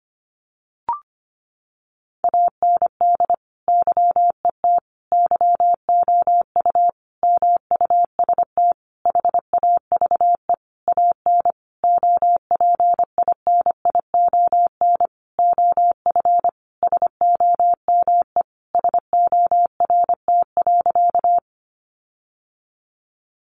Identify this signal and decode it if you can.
Morse code, heard as a keyed tone, about 25 wpm, 700 Hz. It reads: AND YET YOU MUST HAVE AN OPINION OF SOME SORT.
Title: Sentences from Top 1300 Words 25wpm